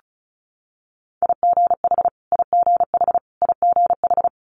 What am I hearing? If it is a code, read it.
Morse code, heard as a keyed tone, about 36 wpm, 700 Hz. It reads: IGH IGH IGH
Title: Multi-letter Phonemes Repeated 3x 35wpm